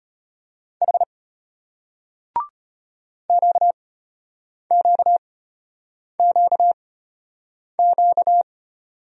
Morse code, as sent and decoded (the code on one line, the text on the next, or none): .... / --.- --.- --.- --.-
H QQQQ